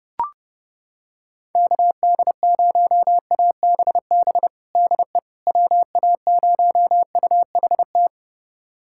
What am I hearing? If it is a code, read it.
Morse code, heard as a keyed tone, about 30 wpm, 700 Hz. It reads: KD0ABB DE WA0UHT
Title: USA Callsign DE Callsign - 6 Characters - 30wpm